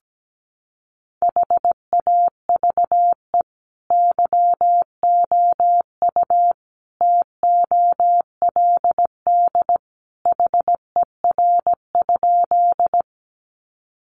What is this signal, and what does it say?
Morse code, heard as a keyed tone, about 17 wpm, 700 Hz. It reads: HAVE YOU TOLD HER?
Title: Sentences from Top 400 Words 17wpm